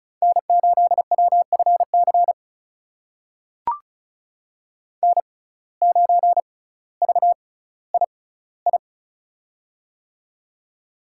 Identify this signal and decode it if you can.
Morse code, heard as a keyed tone, about 36 wpm, spaced as for 18 wpm, 700 Hz. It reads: N8WFC N9VII
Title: USA Call Signs - 5 Characters - 6x Character Spacing - 35wpm